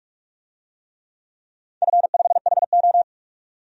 Morse code, heard as a keyed tone, about 46 wpm, 700 Hz. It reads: F5HO
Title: DXCC-100 Call Signs - 4 Characters - 45wpm